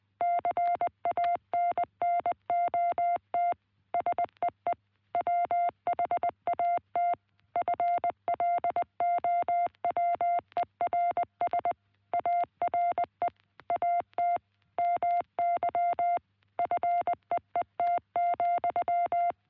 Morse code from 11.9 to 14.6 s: ARE AT